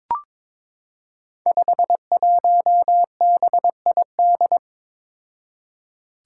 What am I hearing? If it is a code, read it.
Morse code, heard as a keyed tone, about 23 wpm, 700 Hz. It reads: 51BID
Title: Five Challenging Letter-Number Combinations 22wpm